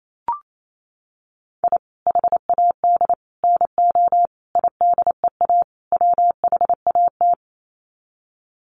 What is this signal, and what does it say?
Morse code, heard as a keyed tone, about 28 wpm, 700 Hz. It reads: I HAD NO IDEA WHAT
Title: Sets of 5 Words Encore 28wpm